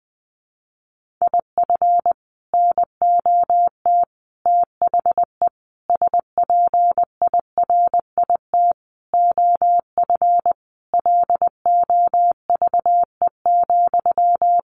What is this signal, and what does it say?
Morse code, heard as a keyed tone, about 20 wpm, 700 Hz. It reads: IF NOT THE SPIRIT OF LOVE,
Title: Sentences from Top 1400 Words 20wpm